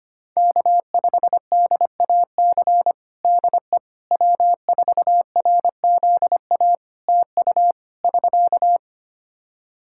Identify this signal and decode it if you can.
Morse code, heard as a keyed tone, about 25 wpm, 700 Hz. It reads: K5DAC DE W4RZA TU <SK>
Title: Practice QSO - Exchange 4 - Ending - 25wpm